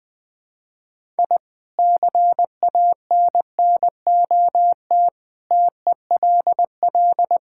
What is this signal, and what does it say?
Morse code, heard as a keyed tone, about 20 wpm, 700 Hz. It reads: I CANNOT TELL
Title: Sentences with 5-Word Phrases - 20wpm